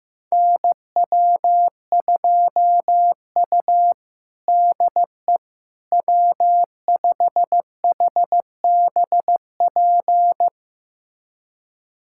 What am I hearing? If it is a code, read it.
Morse code, heard as a keyed tone, about 15 wpm, 700 Hz. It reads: NW2U DE W5HBP
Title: USA Callsign DE Callsign 15wpm